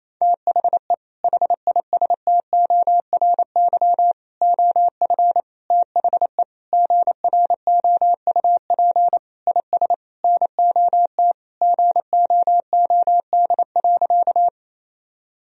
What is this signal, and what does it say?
Morse code, heard as a keyed tone, about 28 wpm, 700 Hz. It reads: THE HISTORY OF THE GROUP IS NOT GOOD.
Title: Sentences from Top 1000 Words - Review Part 1 - 28wpm